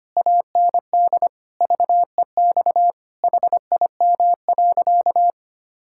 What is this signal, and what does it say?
Morse code, heard as a keyed tone, about 25 wpm, 700 Hz. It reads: AND VEX HIM.
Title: Pangram Challenge v2 - 25wpm